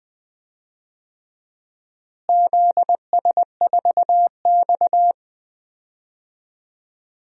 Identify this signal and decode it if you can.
Morse code, heard as a keyed tone, about 21 wpm, 700 Hz. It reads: ZS4X
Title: DXCC-100 Call Signs - 4 Characters - Repeated 3x - 20wpm